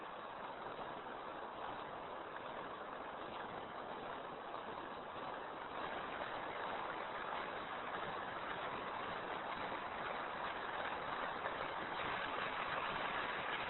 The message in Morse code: none

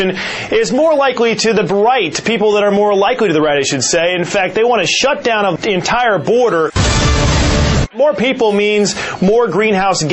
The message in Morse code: none

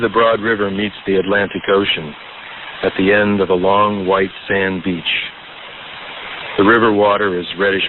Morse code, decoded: none

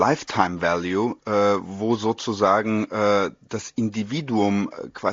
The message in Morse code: none

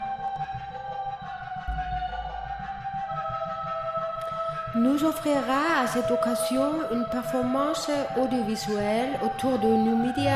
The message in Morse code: none

